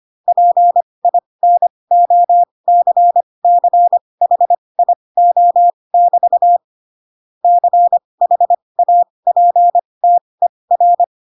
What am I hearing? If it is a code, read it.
Morse code, heard as a keyed tone, about 25 wpm, spaced as for 20 wpm, 700 Hz. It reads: PINOCCHIO<BT> CHAPTER